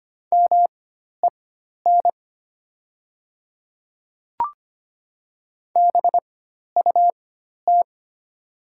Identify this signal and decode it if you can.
Morse code, heard as a keyed tone, about 25 wpm, spaced as for 12 wpm, 700 Hz. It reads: MEN BUT
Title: 4x Character Spacing - Top 200 Words - 25wpm